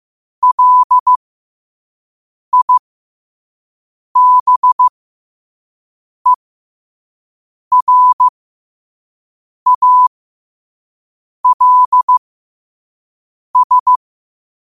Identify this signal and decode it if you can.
Morse code, heard as a keyed tone, about 15 wpm, spaced as for 5 wpm, 1000 Hz. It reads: LIBERALS
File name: qotdfiles / 5wpm